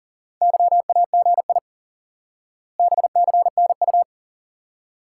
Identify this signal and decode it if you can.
Morse code, heard as a keyed tone, about 37 wpm, 700 Hz. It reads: YAGI BCNU
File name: IQE Course 3 - Lesson 89 - YAGI - Two Elements - 40wpm